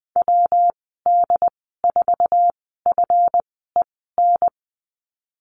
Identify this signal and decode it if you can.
Morse code, heard as a keyed tone, about 20 wpm, spaced as for 14 wpm, 700 Hz. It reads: WD4FEN
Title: 179 Lesson 29g - Slash - Call Signs 20wpm